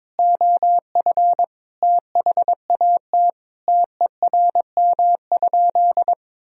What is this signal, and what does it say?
Morse code, heard as a keyed tone, about 22 wpm, 700 Hz. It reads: OF THAT TERM?
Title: Sentences from Top 1000 Words - Review Part 4 - 22wpm